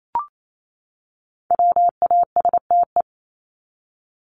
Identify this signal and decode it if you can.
Morse code, heard as a keyed tone, about 28 wpm, 700 Hz. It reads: WASTE
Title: Top 1751-2000 Words 28wpm